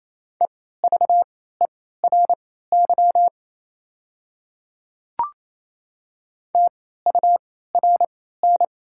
Morse code, heard as a keyed tone, about 27 wpm, spaced as for 16 wpm, 690 Hz. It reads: EVERY TURN
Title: 3x Character Spacing - Top 500 Words - 28wpm